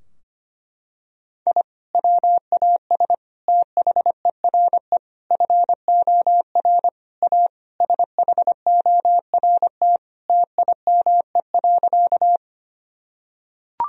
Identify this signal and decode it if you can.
Morse code, heard as a keyed tone, about 25 wpm, 700 Hz. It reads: I WAS THERE FOR A SHORT TIME.
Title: Sentences from Top 1000 Words - Review Part 2 - 25wpm